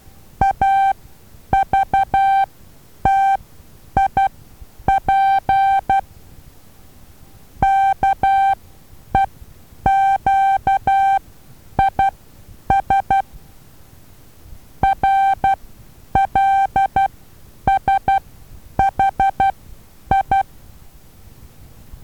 Morse code, heard as a keyed tone, about 12 wpm, spaced as for 9 wpm, 780 Hz. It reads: AVTIP KEQIS RLSHI
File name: Taviro HA6VA B